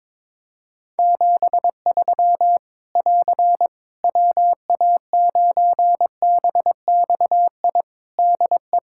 Morse code, heard as a keyed tone, about 22 wpm, 700 Hz. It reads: 73 <AR> WA9BXI DE
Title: K1USN SST QSO Practice 22wpm